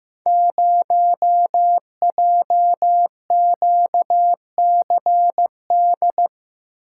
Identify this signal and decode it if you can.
Morse code, heard as a keyed tone, about 15 wpm, 700 Hz. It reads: 0JQCD
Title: Code Groups - 5 Letter:Numbers - 15wpm